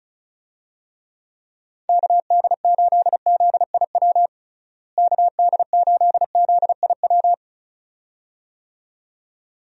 Morse code, heard as a keyed tone, about 34 wpm, 700 Hz. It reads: KD8ZIW KD8ZIW
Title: USA Call Signs - 6 Characters - Repeated 2x - 35wpm